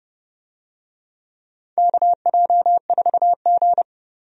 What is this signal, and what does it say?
Morse code, heard as a keyed tone, about 31 wpm, 700 Hz. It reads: KJ4G
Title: USA Call Signs - 4 Characters - 30wpm